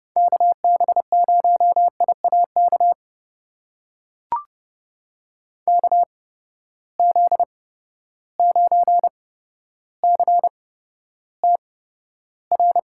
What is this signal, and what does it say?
Morse code, heard as a keyed tone, about 30 wpm, spaced as for 13 wpm, 700 Hz. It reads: KB0IAK KZ9CTR
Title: USA Call Signs - 6 Characters - 8x Character Spacing - 30wpm